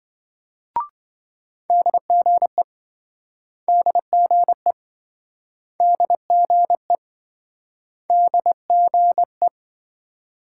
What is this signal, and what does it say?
Morse code, heard as a keyed tone, about 27 wpm, 700 Hz. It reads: DGE DGE DGE DGE